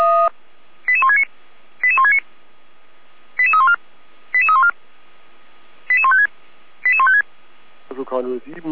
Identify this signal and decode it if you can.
Five-tone selective call (ZVEI), heard as quick runs of five tones, tones of about 70 ms, with speech. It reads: 88169 88169 88314 88314 88156 88156